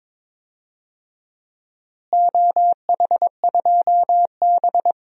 Morse code, heard as a keyed tone, about 22 wpm, 700 Hz. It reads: OH2B
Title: DXCC-100 Call Signs - 4 Characters - Repeated 3x - 22wpm